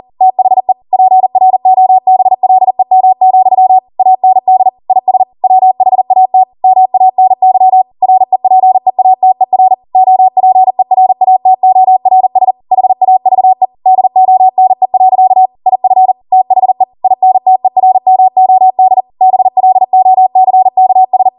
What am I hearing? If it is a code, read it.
Morse code, heard as a keyed tone, about 40 wpm, 750 Hz. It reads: THE PROBLEM, AND IS WHAT MANY REPEATER OPERATORS HAVE DONE. IF THE INTERMOD BLOCKS
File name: Morse / 40WPM